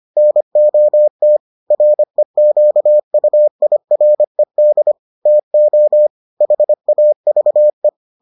Morse code, heard as a keyed tone, about 25 wpm, 600 Hz. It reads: NOT REQUIRED TO HAVE